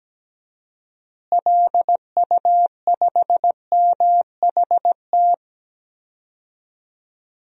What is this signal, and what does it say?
Morse code, heard as a keyed tone, about 17 wpm, 700 Hz. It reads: LU5MHT
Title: DXCC-100 Call Signs - 6 Characters - Repeated 2x - 17wpm